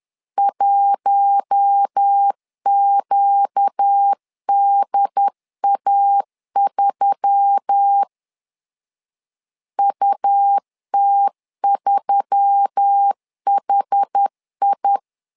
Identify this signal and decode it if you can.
Morse code, heard as a keyed tone, about 10 wpm, 800 Hz. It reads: 1QDA3 UT3HI